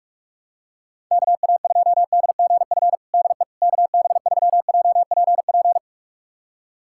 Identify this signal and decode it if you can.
Morse code, heard as a keyed tone, about 46 wpm, 700 Hz. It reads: KA2DGF DE KB3JPP
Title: USA Callsign DE Callsign - 6 Characters - 45wpm